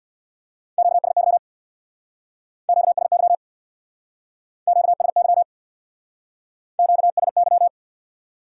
Morse code, heard as a keyed tone, about 60 wpm, 700 Hz. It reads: QSY QSY QSY QSY
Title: Speed-Racing Top 180 QSO Elements 50wpm